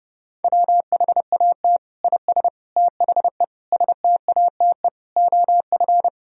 Morse code, heard as a keyed tone, about 32 wpm, 700 Hz. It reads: WHAT IS THE STATE OF
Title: Sentences from Top 1000 Words - Review Part 4 - 30wpm